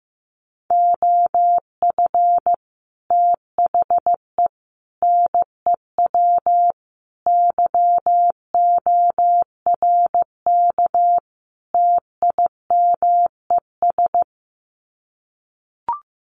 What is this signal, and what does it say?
Morse code, heard as a keyed tone, about 15 wpm, 700 Hz. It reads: OF THE NEW YORK TIMES
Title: Sets of 5 Words 15wpm